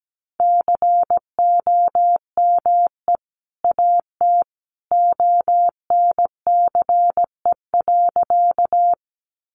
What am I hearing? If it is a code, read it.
Morse code, heard as a keyed tone, about 17 wpm, 700 Hz. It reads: COME AT ONCE.
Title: Sentences from Top 300 Words 17wpm